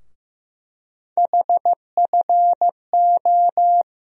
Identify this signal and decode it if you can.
Morse code, heard as a keyed tone, about 15 wpm, 700 Hz. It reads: HFO
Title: USA Call Sign Postfixes - 3 Letters 15wpm